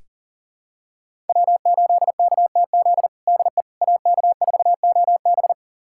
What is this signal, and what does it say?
Morse code, heard as a keyed tone, about 40 wpm, 700 Hz. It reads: W8KTZ DE AK4OB
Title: USA Callsign DE Callsign - 5 Characters - 40wpm